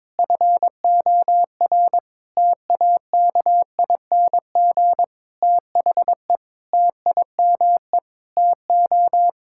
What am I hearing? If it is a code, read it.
Morse code, heard as a keyed tone, about 22 wpm, 700 Hz. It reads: FOR TAKING THE TIME TO
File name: Sets of 9 Words - 22wpm